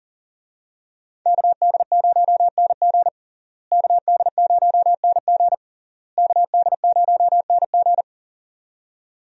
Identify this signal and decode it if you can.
Morse code, heard as a keyed tone, about 40 wpm, 700 Hz. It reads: KD0NG KD0NG KD0NG
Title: USA Call Signs - 5 Characters - Repeated 3x - 40wpm